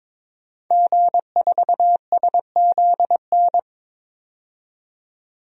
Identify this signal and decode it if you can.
Morse code, heard as a keyed tone, about 22 wpm, 700 Hz. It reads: G4SZN
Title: DXCC-100 Call Signs - 5 Characters - 22wpm